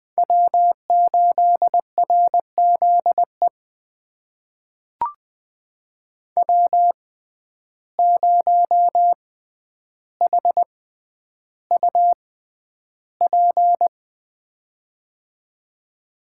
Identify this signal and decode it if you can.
Morse code, heard as a keyed tone, about 20 wpm, spaced as for 10 wpm, 700 Hz. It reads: W8RZE W0HUP
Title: USA Call Signs - 5 Characters - 6x Character Spacing - 20wpm